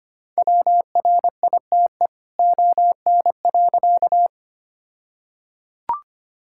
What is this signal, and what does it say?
Morse code, heard as a keyed tone, about 25 wpm, 700 Hz. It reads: WRITE ON.
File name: practice / 25wpm / All-Age Appropriate Jokes - 25wpm